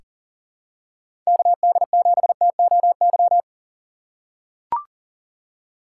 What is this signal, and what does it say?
Morse code, heard as a keyed tone, about 40 wpm, 700 Hz. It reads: KD7TOY
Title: USA Call Signs - 6 Characters - 4x Character Spacing - 40wpm